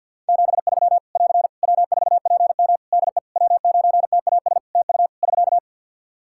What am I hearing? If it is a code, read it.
Morse code, heard as a keyed tone, about 53 wpm, 700 Hz. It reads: 73 <AR> W4PM DE W9TAS TU <SK>
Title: K1USN SST QSO Practice 50wpm